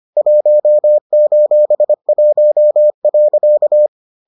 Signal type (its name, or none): Morse code